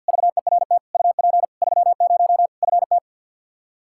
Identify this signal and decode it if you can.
Morse code, heard as a keyed tone, about 47 wpm, 700 Hz. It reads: VERT UP 30 FT
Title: Practice QSO - Exchange 2 - Info - 50wpm